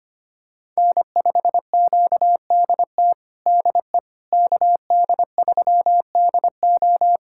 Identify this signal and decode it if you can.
Morse code, heard as a keyed tone, about 25 wpm, 700 Hz. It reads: N5QDT DE KD3DO
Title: USA Callsign DE Callsign - 5 Characters - 25wpm